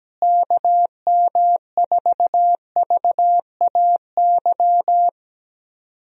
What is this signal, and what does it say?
Morse code, heard as a keyed tone, about 17 wpm, 700 Hz. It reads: KM4VAY
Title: USA Call Signs - 6 Characters - Repeated 2x - 17wpm